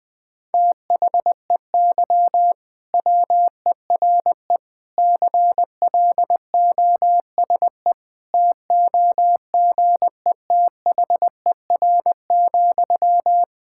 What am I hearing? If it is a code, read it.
Morse code, heard as a keyed tone, about 20 wpm, 700 Hz. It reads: THEY WERE CLOSE TOGETHER,
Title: Sentences from Top 1000 Words - Review Part 6 - 20wpm